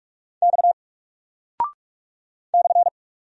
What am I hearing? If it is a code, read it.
Morse code, heard as a keyed tone, about 44 wpm, 700 Hz. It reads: X /
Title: Single Character (Letter:Number:Punctuation:Prosign) 45wpm